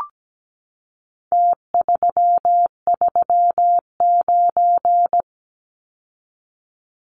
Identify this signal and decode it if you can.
Morse code, heard as a keyed tone, about 17 wpm, 700 Hz. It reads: T339
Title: Numbers - Contest Style - 4-Digit 17wpm